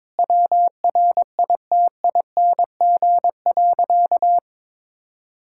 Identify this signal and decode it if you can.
Morse code, heard as a keyed tone, about 22 wpm, 700 Hz. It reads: WRITING.
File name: practice / 22wpm / Sentences with 2-Word Phrases - 22wpm